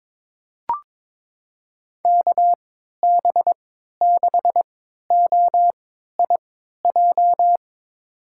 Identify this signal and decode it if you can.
Morse code, heard as a keyed tone, about 22 wpm, spaced as for 13 wpm, 700 Hz. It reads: KB6OIJ
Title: USA Call Signs - 6 Characters - 3x Character Spacing - 22wpm